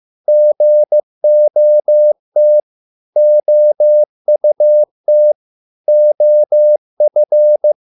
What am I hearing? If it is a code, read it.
Morse code, heard as a keyed tone, about 15 wpm, 600 Hz. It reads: GOT OUT OF